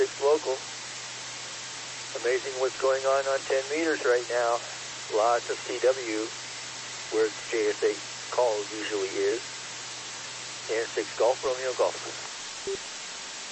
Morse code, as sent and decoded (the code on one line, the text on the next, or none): none